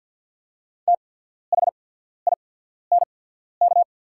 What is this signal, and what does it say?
Morse code, heard as a keyed tone, about 48 wpm, spaced as for 14 wpm, 690 Hz. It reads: THINK
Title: 8x Character Spacing - Top 300 Words - 50wpm